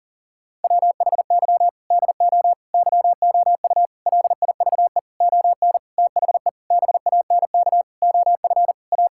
Morse code, heard as a keyed tone, about 40 wpm, 690 Hz. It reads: WHY DO YOU LIVE ON THE BANK OF A
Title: Sentences from Top 1000 Words - Review Part 6 - 40wpm